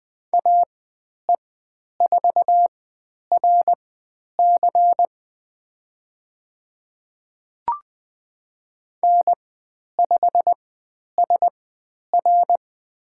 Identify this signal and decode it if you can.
Morse code, heard as a keyed tone, about 20 wpm, spaced as for 10 wpm, 700 Hz. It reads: AE4RC N5SR
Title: 102 Lesson 17g - Letter u - Call Signs 20wpm